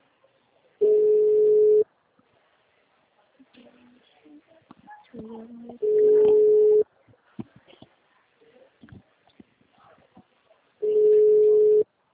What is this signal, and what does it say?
Ringback tone (Europe, 425 Hz), with speech.